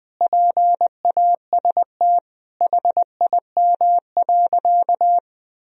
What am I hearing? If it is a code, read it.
Morse code, heard as a keyed tone, about 20 wpm, 700 Hz. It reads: PAST HIM.
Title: Sentences from Top 1000 Words - Review Part 3 - 20wpm